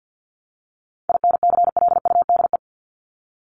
Morse code, heard as a keyed tone, about 50 wpm, 700 Hz. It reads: INCLUDE
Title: Top 501-750 Words 50wpm